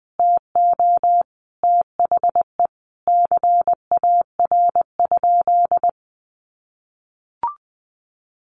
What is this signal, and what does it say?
Morse code, heard as a keyed tone, about 20 wpm, 700 Hz. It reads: TO THE CAR?
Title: Sentences from Top 1200 Words 20wpm